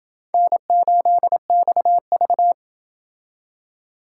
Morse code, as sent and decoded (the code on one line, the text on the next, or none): -. ---.. -..- ...-
N8XV